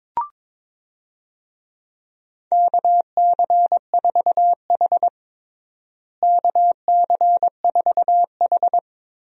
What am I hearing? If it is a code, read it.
Morse code, heard as a keyed tone, about 23 wpm, 700 Hz. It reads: KC4H KC4H
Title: USA Call Signs - 4 Characters - Repeated 2x - 22wpm